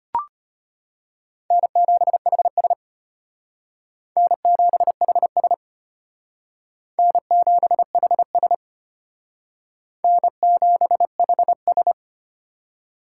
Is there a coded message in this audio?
Yes: Morse code from 1.5 to 11.9 s, heard as a keyed tone, about 32 wpm, 700 Hz. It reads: N7HS N7HS N7HS N7HS